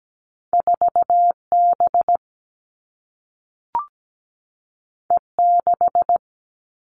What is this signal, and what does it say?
Morse code, heard as a keyed tone, about 17 wpm, 700 Hz. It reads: 4B E6
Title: All Cut Numbers - 2-Digit 17wpm